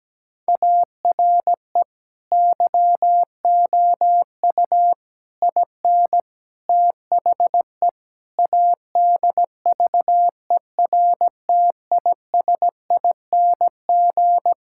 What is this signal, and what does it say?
Morse code, heard as a keyed tone, about 17 wpm, 700 Hz. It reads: ARE YOU IN THE ADVERTISING
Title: Sentences from Top 1000 Words - Review Part 6 - 17wpm